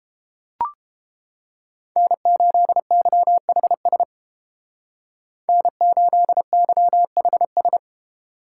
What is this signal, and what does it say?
Morse code, heard as a keyed tone, about 31 wpm, 700 Hz. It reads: N8YHS N8YHS